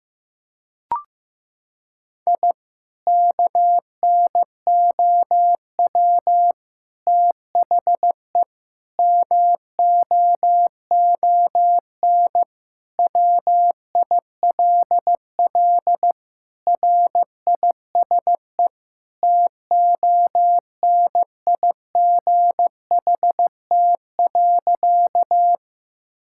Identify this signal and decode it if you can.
Morse code, heard as a keyed tone, about 15 wpm, 700 Hz. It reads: I KNOW THE MOON WILL RISE TONIGHT.
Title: Sentences from Top 1000 Words - Review Part 6 - 15wpm